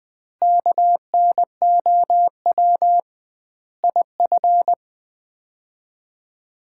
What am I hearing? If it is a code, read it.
Morse code, heard as a keyed tone, about 20 wpm, 700 Hz. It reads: KNOW IF
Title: 2x Word-Spacing - Sets of 5 Words - 20wpm